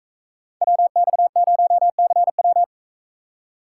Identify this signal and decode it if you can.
Morse code, heard as a keyed tone, about 44 wpm, 700 Hz. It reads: WX0CW